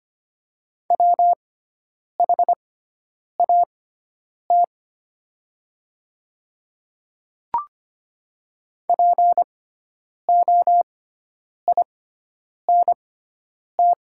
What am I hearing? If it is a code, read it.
Morse code, heard as a keyed tone, about 25 wpm, spaced as for 9 wpm, 700 Hz. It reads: WHAT POINT
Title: 6x Character Spacing - Top 300 Words - 25wpm